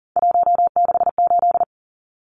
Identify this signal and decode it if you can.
Morse code, heard as a keyed tone, about 40 wpm, 700 Hz. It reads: 168
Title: Numbers - 3-Digit 40wpm